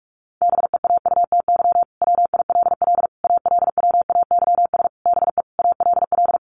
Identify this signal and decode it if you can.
Morse code, heard as a keyed tone, about 44 wpm, 700 Hz. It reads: BEAUTY WILL ALWAYS BE ALL
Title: Sentences from Top 1300 Words 45wpm